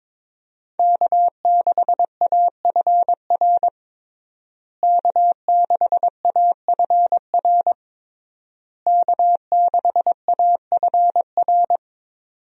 Morse code, heard as a keyed tone, about 21 wpm, 700 Hz. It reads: K6AFR K6AFR K6AFR
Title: USA Call Signs - 5 Characters - Repeated 3x - 22wpm